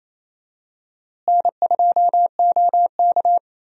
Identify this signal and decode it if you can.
Morse code, heard as a keyed tone, about 28 wpm, 700 Hz. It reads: N2OK
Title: USA Call Signs - 4 Characters - Repeated 3x - 28wpm